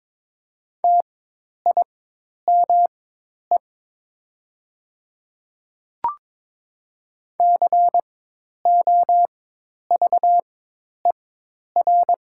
Morse code, heard as a keyed tone, about 22 wpm, spaced as for 10 wpm, 700 Hz. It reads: TIME COVER
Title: 4x Character Spacing - Top 500 Words - 22wpm